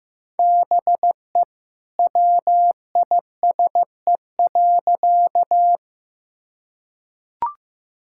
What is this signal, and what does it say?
Morse code, heard as a keyed tone, about 15 wpm, 700 Hz. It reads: BE WISE.
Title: Sentences from Top 1600 Words 15wpm